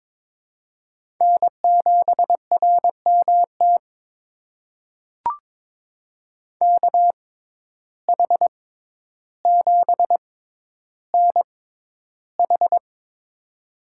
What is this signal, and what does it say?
Morse code, heard as a keyed tone, about 22 wpm, spaced as for 11 wpm, 700 Hz. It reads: N7RMT KH7NH
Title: USA Call Signs - 6x Character Spacing - 22wpm